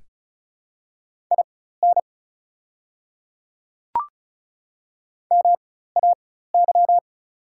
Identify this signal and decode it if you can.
Morse code, heard as a keyed tone, about 33 wpm, spaced as for 16 wpm, 690 Hz. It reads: IN MAY
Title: 4x Character Spacing - Top 100 Words - 35wpm